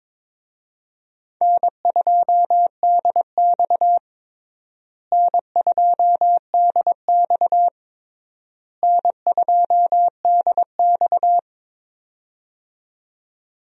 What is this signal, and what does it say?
Morse code, heard as a keyed tone, about 21 wpm, 700 Hz. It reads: N2DX N2DX N2DX
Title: USA Call Signs - 4 Characters - Repeated 3x - 22wpm